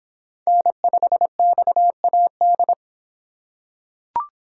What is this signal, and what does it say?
Morse code, heard as a keyed tone, about 24 wpm, 700 Hz. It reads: N5XAD